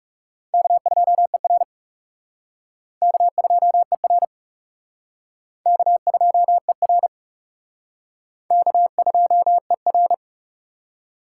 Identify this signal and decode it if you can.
Morse code, heard as a keyed tone, about 40 wpm, 700 Hz. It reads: K2ER K2ER K2ER K2ER